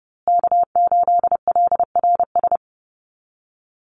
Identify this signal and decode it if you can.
Morse code, heard as a keyed tone, about 28 wpm, 700 Hz. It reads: K8LRS
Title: USA Call Signs Repeated 2x 30wpm